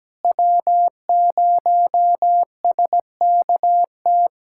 Morse code, heard as a keyed tone, about 17 wpm, 700 Hz. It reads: W0SKT
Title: USA Callsign DE Callsign - 5 Characters - 17wpm